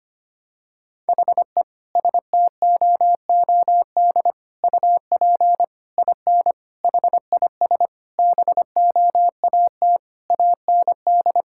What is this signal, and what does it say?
Morse code, heard as a keyed tone, about 25 wpm, 700 Hz. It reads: HE STOOD UP IN HIS BOAT AND
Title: Sentences from Top 1600 Words 25wpm